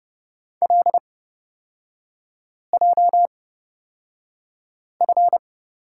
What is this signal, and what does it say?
Morse code, heard as a keyed tone, about 30 wpm, spaced as for 5 wpm, 700 Hz. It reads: LJF
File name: Lesson 36e - BT - Three Characters 30wpm